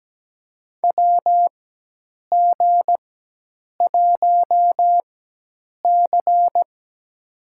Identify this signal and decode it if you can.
Morse code, heard as a keyed tone, about 17 wpm, spaced as for 8 wpm, 700 Hz. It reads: WG1C
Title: USA Call Signs - 4 Characters - 4x Character Spacing - 17wpm